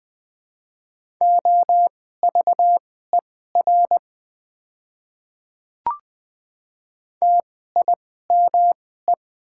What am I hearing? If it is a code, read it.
Morse code, heard as a keyed tone, about 20 wpm, spaced as for 14 wpm, 700 Hz. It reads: OVER TIME
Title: 2x Character Spacing - Top 100 Words - 20wpm